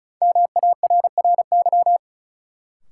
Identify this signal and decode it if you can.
Morse code, heard as a keyed tone, about 36 wpm, 700 Hz. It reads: MARRY
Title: Sentences from Top 1200 Words - New Word Drill Alt - 35wpm